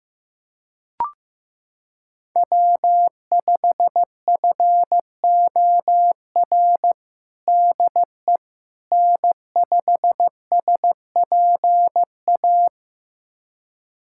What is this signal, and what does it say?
Morse code, heard as a keyed tone, about 15 wpm, 700 Hz. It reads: W5FOR DE N5SPA
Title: USA Callsign DE Callsign - 5 Characters - 15wpm